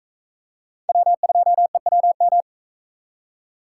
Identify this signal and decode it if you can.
Morse code, heard as a keyed tone, about 43 wpm, 700 Hz. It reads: W2EWM